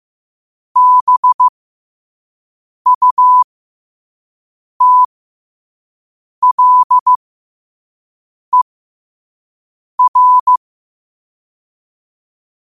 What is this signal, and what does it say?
Morse code, heard as a keyed tone, about 15 wpm, spaced as for 5 wpm, 1000 Hz. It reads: BUTLER
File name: qotdfiles / 5wpm